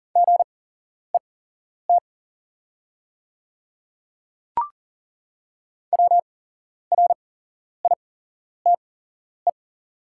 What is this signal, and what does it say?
Morse code, heard as a keyed tone, about 37 wpm, spaced as for 11 wpm, 690 Hz. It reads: GET WRITE
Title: 8x Character Spacing - Top 200 Words - 40wpm